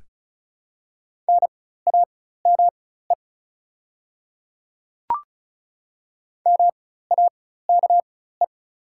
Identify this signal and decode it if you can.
Morse code, heard as a keyed tone, about 33 wpm, spaced as for 16 wpm, 690 Hz. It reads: NAME MAKE
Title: 4x Character Spacing - Top 300 Words - 35wpm